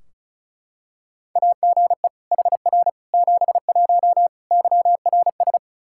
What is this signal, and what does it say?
Morse code, heard as a keyed tone, about 34 wpm, 700 Hz. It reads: AGE HR 71 YRS
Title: Practice QSO - Exchange 3 - More Info - 35wpm